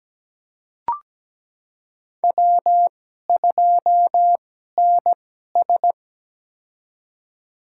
Morse code, heard as a keyed tone, about 17 wpm, spaced as for 12 wpm, 700 Hz. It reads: W2NS
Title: USA Call Signs - 4 Characters - 2x Character Spacing - 17wpm